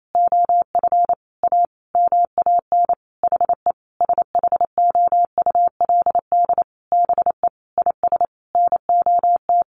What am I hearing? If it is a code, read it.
Morse code, heard as a keyed tone, about 28 wpm, 700 Hz. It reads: OF A MAN HE SHOULD BE IS NOT
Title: Sentences from Top 1300 Words 28wpm